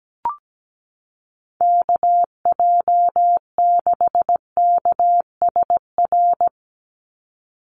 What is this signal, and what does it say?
Morse code, heard as a keyed tone, about 17 wpm, 700 Hz. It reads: KJ6KSR
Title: USA Call Signs 17wpm